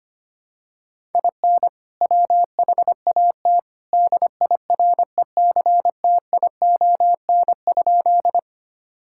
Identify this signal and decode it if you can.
Morse code, heard as a keyed tone, about 25 wpm, 700 Hz. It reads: IN WHAT DIRECTION?
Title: Sentences from Top 1000 Words - Review Part 5 - 25wpm